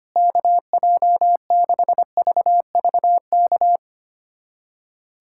Morse code, heard as a keyed tone, about 25 wpm, 700 Hz. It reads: KJ6VVK
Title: USA Callsign DE Callsign - 6 Characters - 25wpm